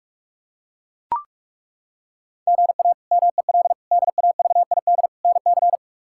Morse code, heard as a keyed tone, about 41 wpm, 700 Hz. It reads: GA MEL DAVID NC
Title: K1USN SST QSO Practice - 45wpm